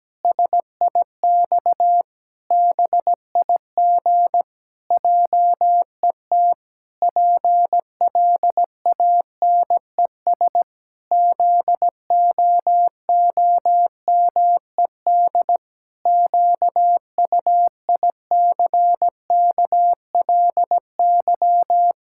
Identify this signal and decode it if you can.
Morse code, heard as a keyed tone, about 17 wpm, 700 Hz. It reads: SIX BIG JET PLANES ZOOMED QUICKLY
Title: Pangram Challenge v2 - 17wpm